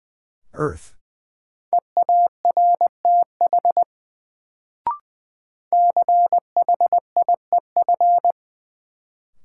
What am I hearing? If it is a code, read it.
Morse code, heard as a keyed tone, about 20 wpm, 700 Hz. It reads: EARTH CHIEF
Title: Sets of 5 Words Encore - New Word Drill - 20wpm